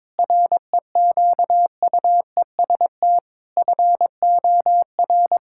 Morse code, heard as a keyed tone, about 21 wpm, 700 Hz. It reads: REQUEST FOR
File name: practice / 22wpm / Sets of 8 Words - 22wpm